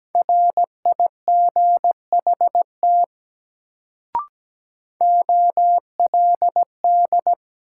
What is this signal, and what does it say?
Morse code, heard as a keyed tone, about 17 wpm, 700 Hz. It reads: RIGHT OLD
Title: Sentences from Top 200 Words - New Word Drill - 17wpm